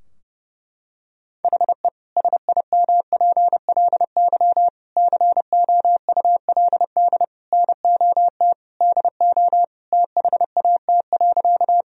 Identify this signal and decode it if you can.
Morse code, heard as a keyed tone, about 30 wpm, 700 Hz. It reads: HE SIMPLY COULD NOT DO THAT.
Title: Sentences from Top 1000 Words - Review Part 4 - 30wpm